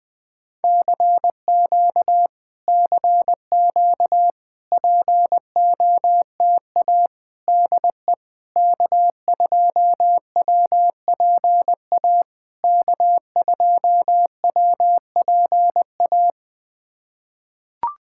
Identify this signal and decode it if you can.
Morse code, heard as a keyed tone, about 21 wpm, 700 Hz. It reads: CQ CQ POTA DE K2WPA K2WPA
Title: POTA QSO Practice - 20wpm